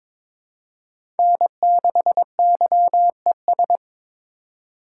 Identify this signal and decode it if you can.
Morse code, heard as a keyed tone, about 21 wpm, 700 Hz. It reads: N6YES
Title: USA Call Signs - 8x Character Spacing - 22wpm